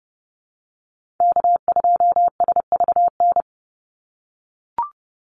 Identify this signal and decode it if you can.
Morse code, heard as a keyed tone, about 30 wpm, 700 Hz. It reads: K2SVN
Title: USA Call Signs 30wpm